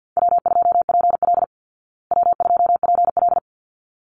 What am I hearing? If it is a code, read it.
Morse code, heard as a keyed tone, about 49 wpm, 700 Hz. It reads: W2PL W2PL
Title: Call Signs Repeated 3x - 50wpm